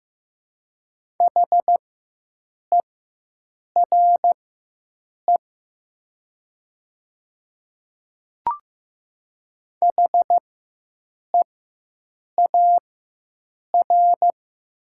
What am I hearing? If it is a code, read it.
Morse code, heard as a keyed tone, about 15 wpm, spaced as for 7 wpm, 700 Hz. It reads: HERE HEAR
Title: 4x Character Spacing - Top 300 Words - 15wpm